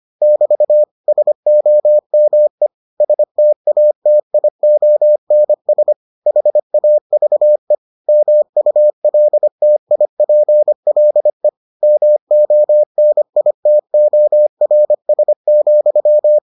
Morse code, heard as a keyed tone, about 25 wpm, 600 Hz. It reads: <BT> SOME STATIONS HAVE MULTIPLE MONITORS,